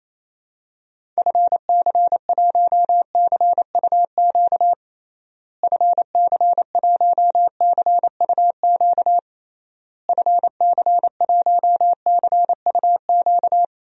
Morse code, heard as a keyed tone, about 28 wpm, 700 Hz. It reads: FC1CUQ FC1CUQ FC1CUQ
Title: DXCC-100 Call Signs - 6 Characters - Repeated 3x - 28wpm